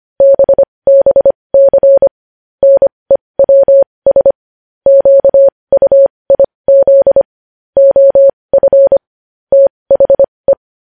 Morse code, heard as a keyed tone, about 25 wpm, spaced as for 20 wpm, 550 Hz. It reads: BBC NEWS QUIZ OF THE
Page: News Headlines in Morse Code at 20 WPM